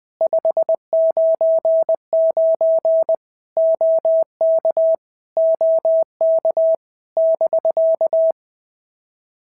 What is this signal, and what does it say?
Morse code, heard as a keyed tone, about 20 wpm, 650 Hz. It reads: 599 OK OK <BK>
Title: POTA QSO Practice - 20wpm